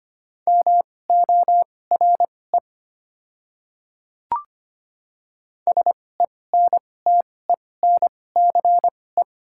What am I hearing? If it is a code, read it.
Morse code, heard as a keyed tone, about 25 wpm, spaced as for 18 wpm, 700 Hz. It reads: MORE SENTENCE
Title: 2x Character Spacing - Top 200 Words - 25wpm